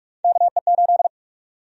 Morse code, heard as a keyed tone, about 42 wpm, 700 Hz. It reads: KE8
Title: USA Call Sign Prefixes - Repeated 2x - 45wpm